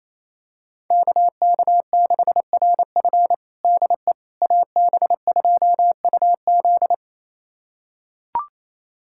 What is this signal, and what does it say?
Morse code, heard as a keyed tone, about 28 wpm, 700 Hz. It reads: KK6RF DE AB2UZ
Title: USA Callsign DE Callsign - 5 Characters - 28wpm